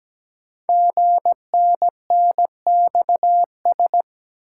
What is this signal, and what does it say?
Morse code, heard as a keyed tone, about 17 wpm, 700 Hz. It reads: GNNXS
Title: Code Groups - 5 Letters - 17wpm